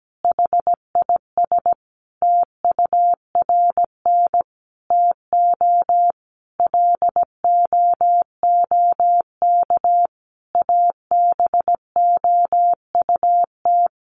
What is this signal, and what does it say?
Morse code, heard as a keyed tone, about 17 wpm, 700 Hz. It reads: HIS TURN TO LOOK ABOUT
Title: Sentences from Top 200 Words 17wpm